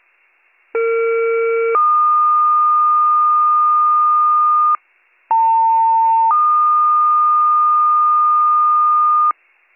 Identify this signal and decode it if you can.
Two-tone page, heard as a one-second tone, then a three-second tone, about 460 then 1220 Hz, about 900 then 1220 Hz. It is heard from 0.7 to 9.3 s.